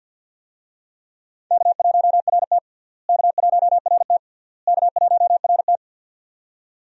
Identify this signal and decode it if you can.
Morse code, heard as a keyed tone, about 52 wpm, 700 Hz. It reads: K1RT K1RT K1RT
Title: USA Call Signs - 4 Characters - Repeated 3x - 50wpm